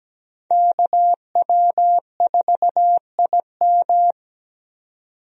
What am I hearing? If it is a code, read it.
Morse code, heard as a keyed tone, about 17 wpm, 700 Hz. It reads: KW4IM
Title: USA Call Signs - 5 Characters - 17wpm